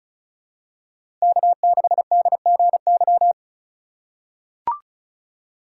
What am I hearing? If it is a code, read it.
Morse code, heard as a keyed tone, about 36 wpm, 700 Hz. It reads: K6DGY
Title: USA Call Signs - 5 Characters - 8x Character Spacing - 35wpm